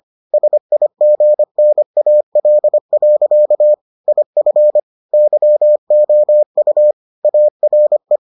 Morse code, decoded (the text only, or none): SIGNAL. IF YOU ARE